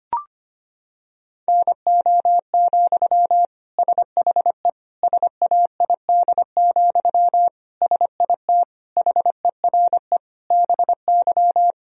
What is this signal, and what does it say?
Morse code, heard as a keyed tone, about 25 wpm, 700 Hz. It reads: NO, SHE SAID, SIT HERE BY
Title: Sentences from Top 1000 Words - Review Part 1 - 25wpm